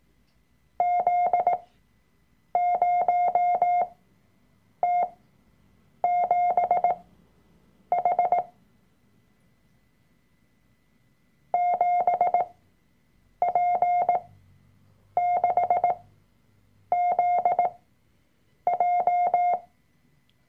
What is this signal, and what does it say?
Morse code, heard as a keyed tone, about 18 wpm, spaced as for 7 wpm, 710 Hz. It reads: Z0T7H 7P6ZJ
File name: Workshop Dateien / CW-Online-Kurs / Lektion 003